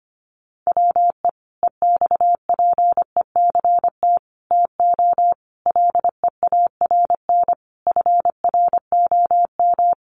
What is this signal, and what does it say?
Morse code, heard as a keyed tone, about 25 wpm, 700 Hz. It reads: WE EXPECT TO LEARN FROM
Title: Sentences from Top 1100 Words 25wpm